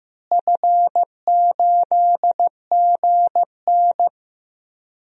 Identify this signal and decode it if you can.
Morse code, heard as a keyed tone, about 15 wpm, 700 Hz. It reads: F8GN
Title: DXCC-100 Call Signs - 4 Characters - Repeated 3x - 15wpm